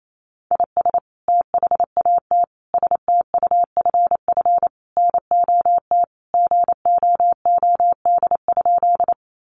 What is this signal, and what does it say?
Morse code, heard as a keyed tone, about 28 wpm, 700 Hz. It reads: IS THAT STUFF NOT GOOD?
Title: Sentences from Top 1400 Words 28wpm